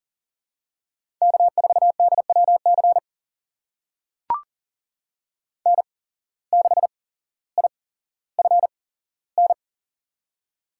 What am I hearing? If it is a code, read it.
Morse code, heard as a keyed tone, about 40 wpm, spaced as for 17 wpm, 700 Hz. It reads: K4DWC N6IFN
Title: USA Call Signs - 5 Characters - 8x Character Spacing - 40wpm